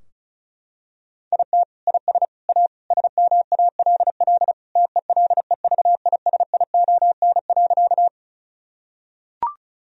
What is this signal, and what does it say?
Morse code, heard as a keyed tone, about 36 wpm, 700 Hz. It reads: IT IS A SMALL TELEVISION.
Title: Sentences from Top 1000 Words - Review Part 4 - 35wpm